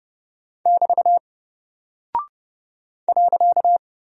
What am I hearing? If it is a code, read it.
Morse code, heard as a keyed tone, about 30 wpm, 700 Hz. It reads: <BT> .
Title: Single Character (Letter:Number:Punctuation:Prosign) Alt 30wpm